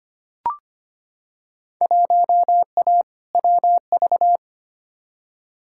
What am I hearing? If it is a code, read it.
Morse code, heard as a keyed tone, about 25 wpm, 700 Hz. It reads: 1A WV
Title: Field Day Exchanges Encore - 25wpm